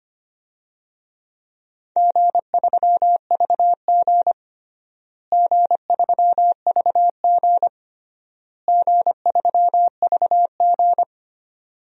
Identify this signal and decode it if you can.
Morse code, heard as a keyed tone, about 25 wpm, 700 Hz. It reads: G3VG G3VG G3VG
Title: DXCC-100 Call Signs - 4 Characters - Repeated 3x - 25wpm